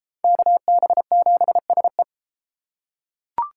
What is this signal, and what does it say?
Morse code, heard as a keyed tone, about 29 wpm, 700 Hz. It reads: KB7SE